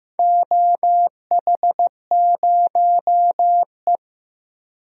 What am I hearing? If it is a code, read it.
Morse code, heard as a keyed tone, about 15 wpm, 700 Hz. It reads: OH0E
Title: DXCC-100 Call Signs - 4 Characters - 15wpm